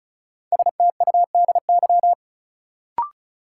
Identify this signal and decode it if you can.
Morse code, heard as a keyed tone, about 36 wpm, 700 Hz. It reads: STUDY
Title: Sets of 5 Words - New Word Drill - 35wpm